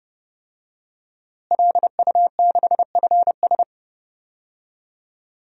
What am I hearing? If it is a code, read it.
Morse code, heard as a keyed tone, about 31 wpm, 700 Hz. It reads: LU6FS
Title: DXCC-100 Call Signs - 5 Characters - Repeated 2x - 30wpm